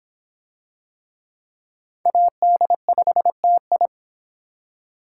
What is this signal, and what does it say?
Morse code, heard as a keyed tone, about 26 wpm, 700 Hz. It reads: AD5TI